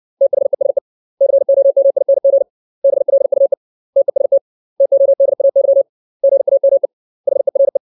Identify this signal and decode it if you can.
Morse code, heard as a keyed tone, about 60 wpm, 550 Hz. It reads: THESE COMING BLUE TEST TODAY GAME HERE
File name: 632 words / 632 Words 60WPM